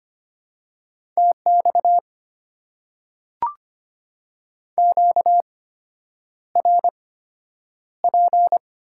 Morse code, heard as a keyed tone, about 25 wpm, spaced as for 9 wpm, 700 Hz. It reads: TX QRP
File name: Top 90 QSO Elements - 8x Character Spacing - 25wpm